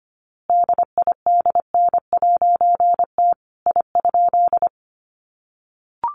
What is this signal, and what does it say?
Morse code, heard as a keyed tone, about 25 wpm, 700 Hz. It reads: DIDN'T I?
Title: A Sound Of Thunder 25wpm